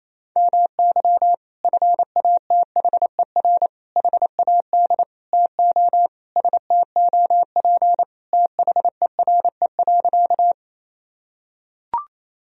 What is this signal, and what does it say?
Morse code, heard as a keyed tone, about 28 wpm, 700 Hz. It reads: MY FATHER HAD TO STOP THERE.
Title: Sentences from Top 1000 Words - Review Part 1 - 28wpm